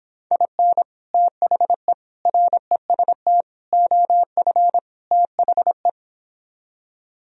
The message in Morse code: IN THE REST OF THE

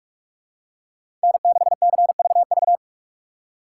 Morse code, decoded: N6CVV

